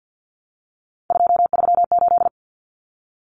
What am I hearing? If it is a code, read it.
Morse code, heard as a keyed tone, about 50 wpm, 700 Hz. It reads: 238
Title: Numbers - 3-Digit 50wpm